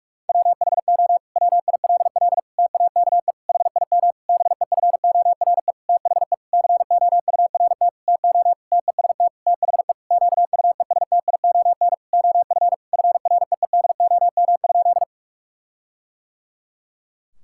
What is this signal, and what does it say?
Morse code, heard as a keyed tone, about 46 wpm, 700 Hz. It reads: WHO WILL TAKE HIM BEFORE THE COURT TO TEST THE QUESTION OF FREEDOM?
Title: Sentences from Top 1600 Words 45wpm